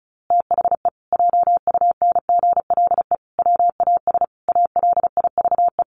Morse code, heard as a keyed tone, about 34 wpm, 700 Hz. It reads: THE JUNGLE WAS ALIVE
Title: A Sound Of Thunder 35wpm